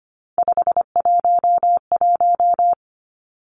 Morse code, heard as a keyed tone, about 25 wpm, 700 Hz. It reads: 511
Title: Three Challenging Letter-Number Combinations 25wpm